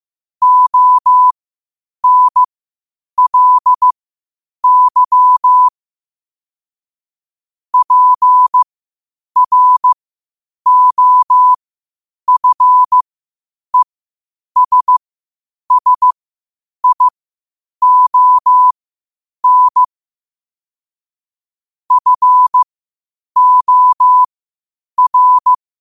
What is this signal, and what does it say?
Morse code, heard as a keyed tone, about 15 wpm, spaced as for 8 wpm, 1000 Hz. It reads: ONLY PROFESSION FOR
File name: qotdfiles / 8wpm